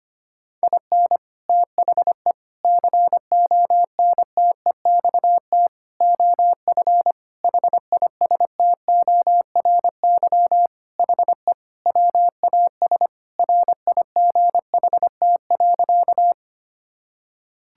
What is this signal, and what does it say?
Morse code, heard as a keyed tone, about 25 wpm, 700 Hz. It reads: IN THE CONTEXT OF HISTORY HE WAS RIGHT.
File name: practice / 25wpm / Sentences with 4-Word Phrases - 25wpm